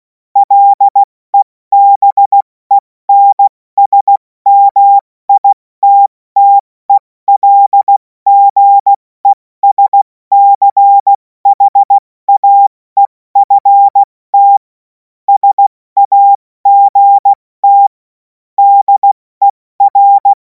Morse code, heard as a keyed tone, about 16 wpm, 800 Hz. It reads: LEBENSMITTELGESCHAEFT SAGT DER
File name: Morsekurs / Witze / Witz 02. Juni 2026 70 F80